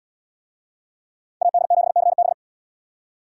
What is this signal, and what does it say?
Morse code, heard as a keyed tone, about 76 wpm, 700 Hz. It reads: AD7CB